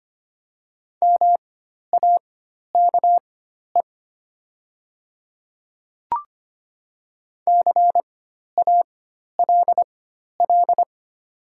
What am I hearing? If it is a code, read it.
Morse code, heard as a keyed tone, about 25 wpm, spaced as for 12 wpm, 700 Hz. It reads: MAKE CALL
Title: 4x Character Spacing - Top 100 Words - 25wpm